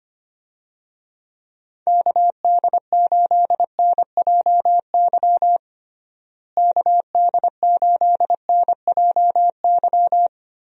Morse code, heard as a keyed tone, about 25 wpm, 700 Hz. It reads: KD8NJY KD8NJY
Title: USA Call Signs - 6 Characters - Repeated 2x - 25wpm